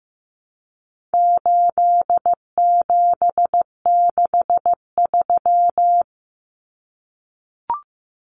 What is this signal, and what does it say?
Morse code, heard as a keyed tone, about 15 wpm, 700 Hz. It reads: 8763
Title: Common Cut Numbers - 4-Digit 15wpm